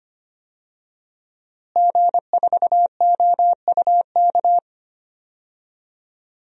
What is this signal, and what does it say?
Morse code, heard as a keyed tone, about 25 wpm, 700 Hz. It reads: G4OUK
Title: DXCC-100 Call Signs - 5 Characters - Repeated 3x - 25wpm